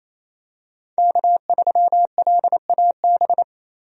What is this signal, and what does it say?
Morse code, heard as a keyed tone, about 28 wpm, 700 Hz. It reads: K3LAB
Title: USA Call Signs - 5 Characters - 28wpm